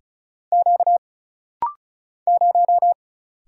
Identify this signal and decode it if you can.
Morse code, heard as a keyed tone, about 35 wpm, 700 Hz. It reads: Q 0
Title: Single Letter-Number Alt - Rapid-Fire 35wpm